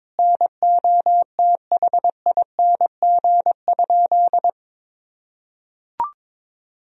Morse code, heard as a keyed tone, about 21 wpm, 700 Hz. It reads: NOTHING?
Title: Sentences from Top 500 Words and Abbreviations 22wpm